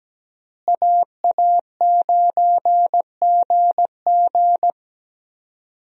Morse code, heard as a keyed tone, about 17 wpm, 700 Hz. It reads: AA9GG
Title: USA Call Signs - 5 Characters - Repeated 2x - 17wpm